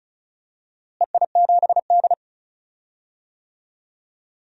Morse code, heard as a keyed tone, about 36 wpm, 700 Hz. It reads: EI7D
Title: DXCC-100 Call Signs - 4 Characters - Repeated 2x - 35wpm